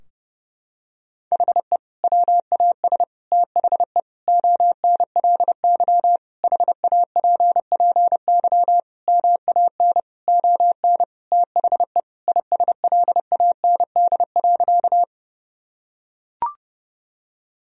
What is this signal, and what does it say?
Morse code, heard as a keyed tone, about 30 wpm, 700 Hz. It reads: HE WAS THE ONLY HAPPY MAN ON THE ISLAND.
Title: Sentences from Top 1000 Words - Review Part 3 - 30wpm